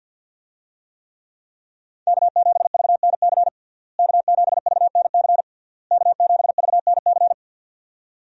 Morse code, heard as a keyed tone, about 50 wpm, 700 Hz. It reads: K7VNC K7VNC K7VNC
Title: USA Call Signs - 5 Characters - Repeated 3x - 50wpm